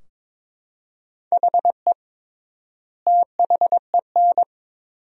Morse code, heard as a keyed tone, about 22 wpm, 700 Hz. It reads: HE THEN
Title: 3x Word-Spacing - Sentences from Top 300 Words - 22wpm